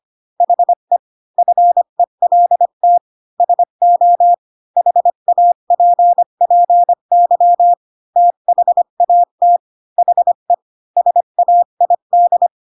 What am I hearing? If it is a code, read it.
Morse code, heard as a keyed tone, about 25 wpm, 700 Hz. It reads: HE FELT SO HAPPY THAT HE SAID
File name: Pinocchio / Pinocchio ch8 20 25 ik1hge